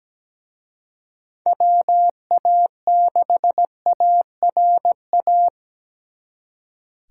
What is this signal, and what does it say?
Morse code, heard as a keyed tone, about 20 wpm, 700 Hz. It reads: WA6ARA
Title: Speed Racing - USA 2x3 Call Signs 17wpm